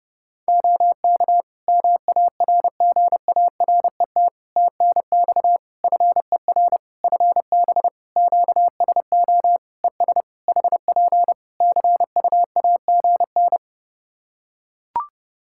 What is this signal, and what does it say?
Morse code, heard as a keyed tone, about 30 wpm, 700 Hz. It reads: OK MARGARET TNX FER FB QSO ES HP CUAGN
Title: Practice QSO - Exchange 4 - Ending - 30wpm